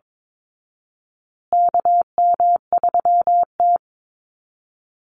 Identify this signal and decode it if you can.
Morse code, heard as a keyed tone, about 22 wpm, 700 Hz. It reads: KM3T
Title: Call Signs Repeated 2x - 22wpm